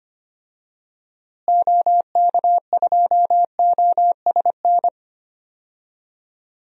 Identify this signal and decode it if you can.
Morse code, heard as a keyed tone, about 25 wpm, 700 Hz. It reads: OK2OSN
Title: DXCC-100 Call Signs - 6 Characters - Repeated 3x - 25wpm